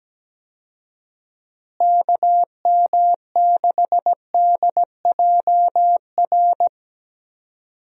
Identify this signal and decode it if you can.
Morse code, heard as a keyed tone, about 17 wpm, 700 Hz. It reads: KM6DJR
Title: USA Call Signs - 6 Characters - Repeated 3x - 17wpm